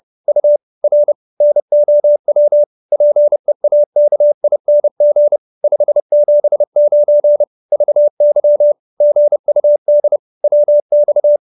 Morse code, heard as a keyed tone, about 30 wpm, 590 Hz. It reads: U R NOW PEAKING 579 VY GUD WX